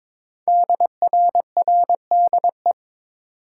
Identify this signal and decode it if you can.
Morse code, heard as a keyed tone, about 23 wpm, 700 Hz. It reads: DRRDE
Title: Code Groups - 5 Letters - 22wpm